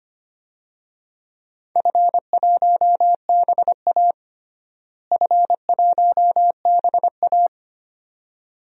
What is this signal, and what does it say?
Morse code, heard as a keyed tone, about 25 wpm, 700 Hz. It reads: F1BA F1BA
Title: DXCC-100 Call Signs - 4 Characters - Repeated 2x - 25wpm